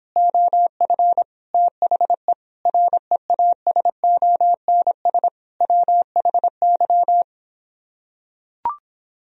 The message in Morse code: OF THE REASONS WHY

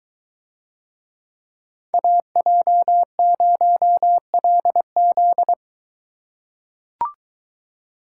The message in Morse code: AJ0LZ

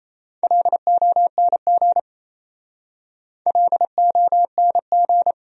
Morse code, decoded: LONG LONG

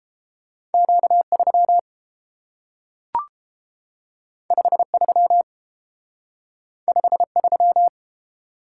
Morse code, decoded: Q3 53 53

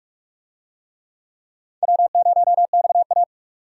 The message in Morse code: W0XA

